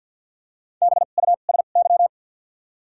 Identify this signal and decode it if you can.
Morse code, heard as a keyed tone, about 53 wpm, spaced as for 37 wpm, 700 Hz. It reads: BUSY
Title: 2x Character Spacing - Top 500 Words - 50wpm